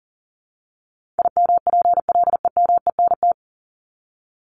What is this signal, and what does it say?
Morse code, heard as a keyed tone, about 40 wpm, 700 Hz. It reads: IMPLEMENT
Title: Top 1501-1750 Words 40wpm